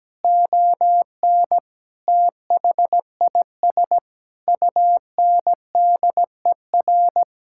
Morse code, heard as a keyed tone, about 17 wpm, 700 Hz. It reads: ON THIS UNDER
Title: Sentences from Top 1000 Words - Review Part 6 - 17wpm